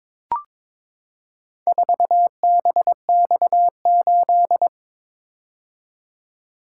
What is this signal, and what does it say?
Morse code, heard as a keyed tone, about 22 wpm, 700 Hz. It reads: 4BX8
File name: practice / 22wpm / Four Letter-Number Combinations - 22wpm